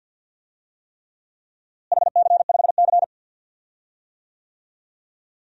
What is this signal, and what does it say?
Morse code, heard as a keyed tone, about 45 wpm, 700 Hz. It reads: HC5C
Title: DXCC-100 Call Signs - 4 Characters - Repeated 3x - 50wpm